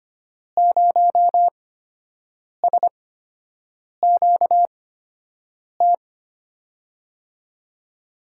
Morse code, heard as a keyed tone, about 25 wpm, spaced as for 7 wpm, 700 Hz. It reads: 0SQT
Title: USA Call Signs - 6 Characters - 8x Character Spacing - 25wpm